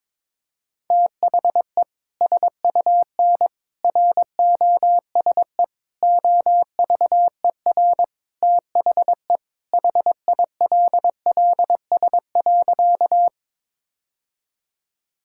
Morse code, heard as a keyed tone, about 22 wpm, 700 Hz. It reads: THE SUN ROSE OVER THE HILLS.
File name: practice / 22wpm / Sentences with 2-Word Phrases - 22wpm